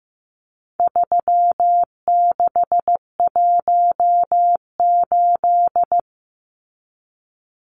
Morse code, heard as a keyed tone, about 15 wpm, 700 Hz. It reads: 3618
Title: Numbers - 4-Digit 15wpm